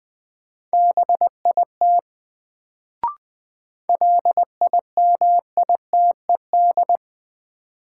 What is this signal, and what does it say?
Morse code, heard as a keyed tone, about 20 wpm, 700 Hz. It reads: BIT LIMITED
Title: Sentences from Top 900 Words - New Word Drill - 20wpm